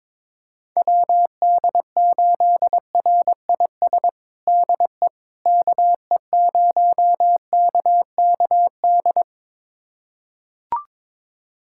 Morse code, heard as a keyed tone, about 22 wpm, 700 Hz. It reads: WD8RIS DE KE0KKD
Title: USA Callsign DE Callsign - 6 Characters - 22wpm